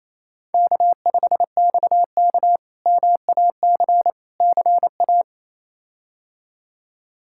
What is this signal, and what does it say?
Morse code, heard as a keyed tone, about 28 wpm, 700 Hz. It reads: K5XK MAC CA
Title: K1USN SST QSO Practice - 28wpm